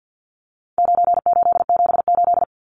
Morse code, heard as a keyed tone, about 50 wpm, 700 Hz. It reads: 9878
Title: Numbers - 4-Digit 50wpm